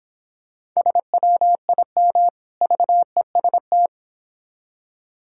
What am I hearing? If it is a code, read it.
Morse code, heard as a keyed tone, about 26 wpm, 700 Hz. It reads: SWIM VEST